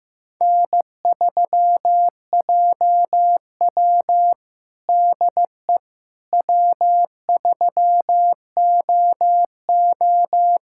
Morse code, heard as a keyed tone, about 15 wpm, 700 Hz. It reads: N3JW DE W3OO
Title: USA Callsign DE Callsign - 4 Characters - 15wpm